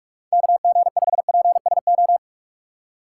Morse code, heard as a keyed tone, about 46 wpm, 700 Hz. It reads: KG5PSO
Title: USA Call Signs - 6 Characters - 4x Character Spacing - 45wpm